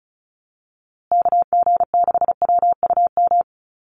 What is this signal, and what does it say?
Morse code, heard as a keyed tone, about 34 wpm, 700 Hz. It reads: KG6WUM
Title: USA Call Signs Repeated 3x 35wpm